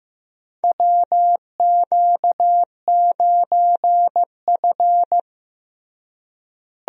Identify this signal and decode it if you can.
Morse code, heard as a keyed tone, about 15 wpm, 700 Hz. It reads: WQ9F
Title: USA Call Signs - 4 Characters - Repeated 2x - 15wpm